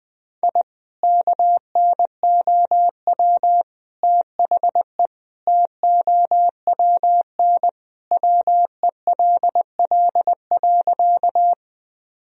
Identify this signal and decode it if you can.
Morse code, heard as a keyed tone, about 20 wpm, 700 Hz. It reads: I KNOW THE TOWN WELL.
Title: Sentences from Top 1000 Words - Review Part 2 - 20wpm